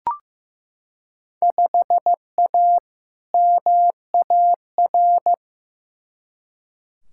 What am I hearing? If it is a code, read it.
Morse code, heard as a keyed tone, about 15 wpm, 700 Hz. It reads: 5A MAR
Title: Field Day Exchanges Encore - 15wpm